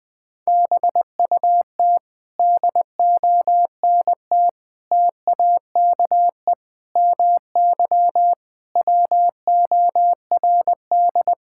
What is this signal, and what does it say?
Morse code, heard as a keyed tone, about 20 wpm, 700 Hz. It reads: BUT DONT TAKE MY WORD
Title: Sentences with 7-Word Phrases - Spoken Beginning - 20wpm